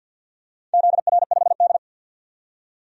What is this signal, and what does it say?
Morse code, heard as a keyed tone, about 51 wpm, 700 Hz. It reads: ZR5D
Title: DXCC-100 Call Signs - 4 Characters - 50wpm